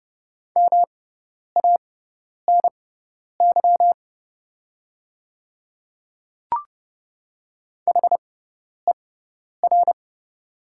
Morse code, heard as a keyed tone, about 30 wpm, spaced as for 10 wpm, 700 Hz. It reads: MANY HER
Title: 6x Character Spacing - Top 100 Words - 30wpm